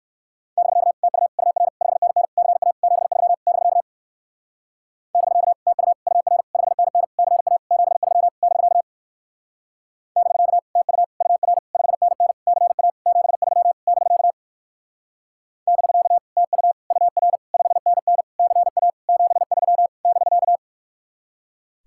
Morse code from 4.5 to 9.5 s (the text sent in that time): <BK> TU UR 5NN CA 73 <BK>